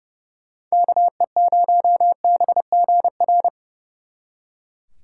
Morse code, heard as a keyed tone, about 28 wpm, 700 Hz. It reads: KE0BGR
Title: Speed Racing - USA 2x3 Call Signs 30wpm